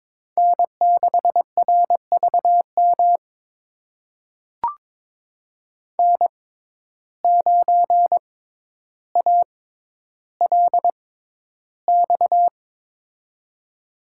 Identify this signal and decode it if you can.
Morse code, heard as a keyed tone, about 22 wpm, spaced as for 11 wpm, 700 Hz. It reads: N6RVM N9ALX
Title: USA Call Signs - 5 Characters - 6x Character Spacing - 22wpm